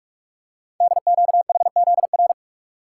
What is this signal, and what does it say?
Morse code, heard as a keyed tone, about 44 wpm, 700 Hz. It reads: DQHZR
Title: Code Groups - 5 Letter:Numbers - 45wpm